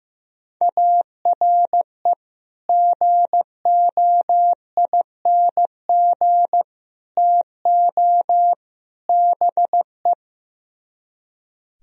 Morse code, heard as a keyed tone, about 16 wpm, 700 Hz. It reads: ARE GOING TO BE
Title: Speed-Racing - Sets of 4 Words 15wpm